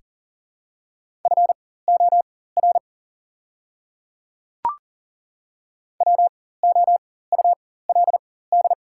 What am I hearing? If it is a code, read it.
Morse code, heard as a keyed tone, about 40 wpm, spaced as for 19 wpm, 700 Hz. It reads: FOR WOULD
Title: 4x Character Spacing - Top 200 Words - 40wpm